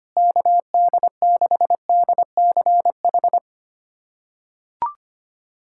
Morse code, heard as a keyed tone, about 25 wpm, 700 Hz. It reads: KD6DCH
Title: USA Callsign DE Callsign - 6 Characters - 25wpm